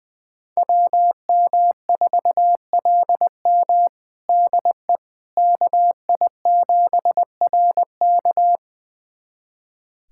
Morse code, heard as a keyed tone, about 20 wpm, 700 Hz. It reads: WM4LM DE KI7RK
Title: USA Callsign DE Callsign - 5 Characters - 20wpm